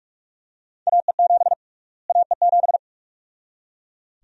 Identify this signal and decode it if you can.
Morse code, heard as a keyed tone, about 41 wpm, 700 Hz. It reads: AE7 AE7
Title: USA Call Sign Prefixes - Repeated 2x - 45wpm